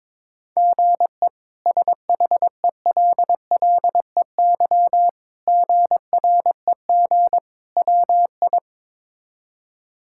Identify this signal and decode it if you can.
Morse code, heard as a keyed tone, about 22 wpm, 700 Hz. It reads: GE SHELLEY GREG WI
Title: K1USN SST QSO Practice - 22wpm